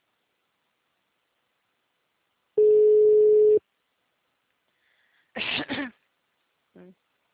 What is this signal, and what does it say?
Ringback tone (Europe, 425 Hz), with speech.